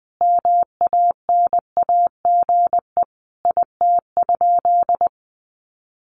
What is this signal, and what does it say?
Morse code, heard as a keyed tone, about 20 wpm, 700 Hz. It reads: MANAGE IT?
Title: Sentences from Top 1400 Words 20wpm